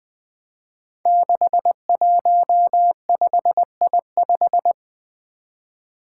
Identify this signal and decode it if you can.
Morse code, heard as a keyed tone, about 20 wpm, 700 Hz. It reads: 615I5
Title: Five Challenging Letter-Number Combinations 20wpm